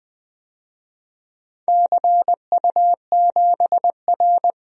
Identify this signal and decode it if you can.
Morse code, heard as a keyed tone, about 20 wpm, 700 Hz. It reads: CU7R
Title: DXCC-100 Call Signs - 4 Characters - Repeated 2x - 20wpm